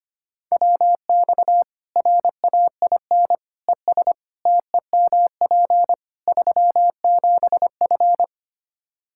Morse code, heard as a keyed tone, about 25 wpm, 700 Hz. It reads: WX RAIN ES TEMP 37F
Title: Practice QSO - Exchange 2 - Info - 25wpm